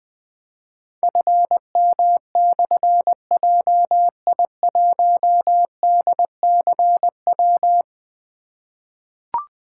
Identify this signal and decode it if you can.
Morse code, heard as a keyed tone, about 20 wpm, 700 Hz. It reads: FM/JI1DCW
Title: Difficult Call Signs 20wpm